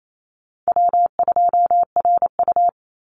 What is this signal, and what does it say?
Morse code, heard as a keyed tone, about 28 wpm, 700 Hz. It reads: W2RU
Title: Call Signs Repeated 3x - 28wpm